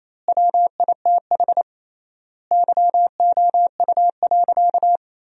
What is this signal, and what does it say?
Morse code, heard as a keyed tone, about 28 wpm, 700 Hz. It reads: WITH YOU.
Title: 3x Word-Spacing - Sentences from Top 300 Words - 28wpm